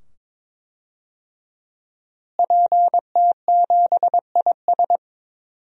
Morse code, heard as a keyed tone, about 21 wpm, 700 Hz. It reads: PT7IS
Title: DXCC-100 Call Signs - 5 Characters - Repeated 2x - 22wpm